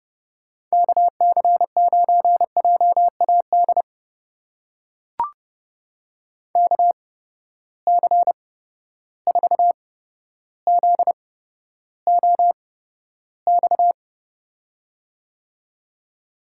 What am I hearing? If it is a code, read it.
Morse code, heard as a keyed tone, about 31 wpm, spaced as for 13 wpm, 700 Hz. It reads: KC9JAD KC4ZOX
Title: USA Call Signs - 6 Characters - 8x Character Spacing - 30wpm